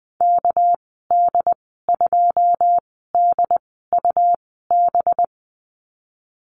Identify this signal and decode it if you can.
Morse code, heard as a keyed tone, about 19 wpm, spaced as for 14 wpm, 700 Hz. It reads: KD2DUB
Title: 238 Lesson 38g - Letter q - Call Signs 20wpm